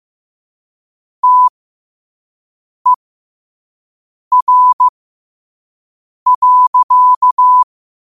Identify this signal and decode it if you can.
Morse code, heard as a keyed tone, about 15 wpm, spaced as for 5 wpm, 1000 Hz. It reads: TER.
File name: qotdfiles / 5wpm